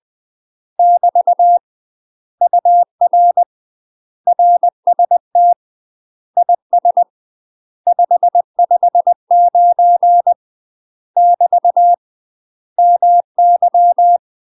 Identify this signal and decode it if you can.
Morse code, heard as a keyed tone, about 20 wpm, 700 Hz. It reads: <BT> UR RST IS 559 <BT> MY
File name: UK MASTER QSOs TRAINING / 20WPM / UK MASTER QSOs 0067